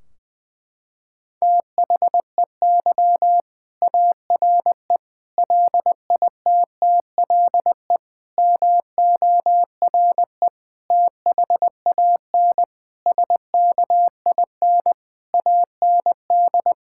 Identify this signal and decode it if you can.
Morse code, heard as a keyed tone, about 20 wpm, 700 Hz. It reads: THEY ARE LITTLE MORE THAN SKIN AND
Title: Sentences from Top 1600 Words 20wpm